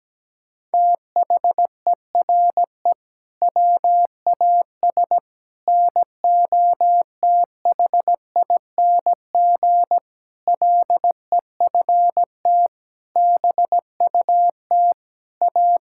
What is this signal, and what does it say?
Morse code, heard as a keyed tone, about 17 wpm, 700 Hz. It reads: THERE WAS NOTHING LEFT BUT A
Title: Sentences from Top 1600 Words 17wpm